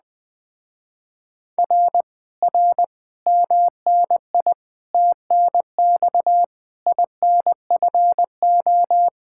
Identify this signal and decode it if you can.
Morse code, heard as a keyed tone, about 20 wpm, 700 Hz. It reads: R R MNI TNX INFO
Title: Speed-Racing Common QSO Phrases Encore - 17wpm